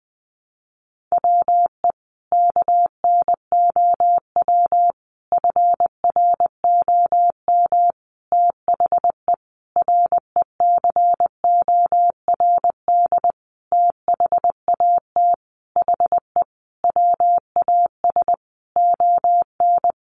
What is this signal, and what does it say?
Morse code, heard as a keyed tone, about 20 wpm, 700 Hz. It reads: WE KNOW FROM THE RECORD THAT HE WAS ON
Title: Sentences from Top 500 Words 20wpm